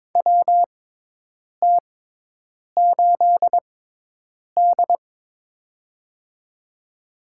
Morse code, heard as a keyed tone, about 22 wpm, spaced as for 8 wpm, 700 Hz. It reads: WT8D
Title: USA Call Signs - 4 Characters - 6x Character Spacing - 22wpm